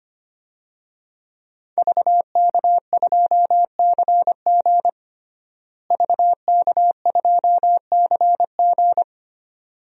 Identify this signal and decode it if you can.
Morse code, heard as a keyed tone, about 25 wpm, 700 Hz. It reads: VK2CG VK2CG
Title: DXCC-100 Call Signs - 5 Characters - Repeated 2x - 25wpm